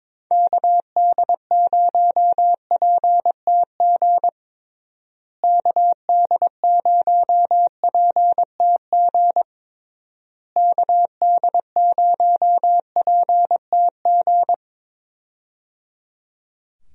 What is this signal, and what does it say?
Morse code, heard as a keyed tone, about 22 wpm, 700 Hz. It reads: KD0PTG KD0PTG KD0PTG
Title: USA Call Signs - 6 Characters - Repeated 3x - 22wpm